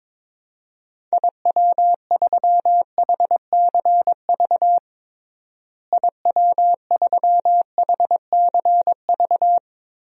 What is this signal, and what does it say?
Morse code, heard as a keyed tone, about 22 wpm, 700 Hz. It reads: IW3HCV IW3HCV
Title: DXCC-100 Call Signs - 6 Characters - Repeated 3x - 22wpm